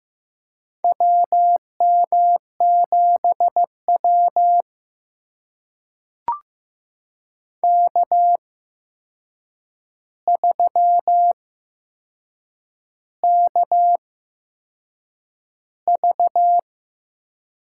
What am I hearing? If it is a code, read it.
Morse code, heard as a keyed tone, about 15 wpm, spaced as for 6 wpm, 700 Hz. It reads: WM7W K3KV
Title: USA Call Signs - 4 Characters - 8x Character Spacing - 15wpm